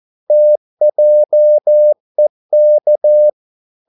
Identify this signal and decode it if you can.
Morse code, heard as a keyed tone, about 14 wpm, 600 Hz. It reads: TJEK
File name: Korte ord DK 14wpm